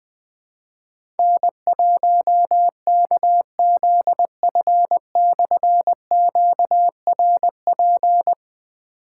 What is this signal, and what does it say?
Morse code, heard as a keyed tone, about 20 wpm, 700 Hz. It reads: N1KZF/QRP
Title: Difficult Call Signs 20wpm